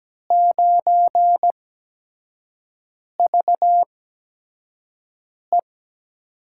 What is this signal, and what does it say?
Morse code, heard as a keyed tone, about 17 wpm, spaced as for 5 wpm, 700 Hz. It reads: 9VE
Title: USA Call Signs - 5 Characters - 8x Character Spacing - 17wpm